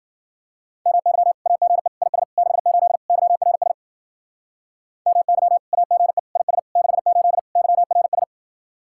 Morse code, heard as a keyed tone, about 49 wpm, 700 Hz. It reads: MY AGE IS 68 YRS MY AGE IS 68 YRS
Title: Speed-Racing Common QSO Phrases Encore - 40wpm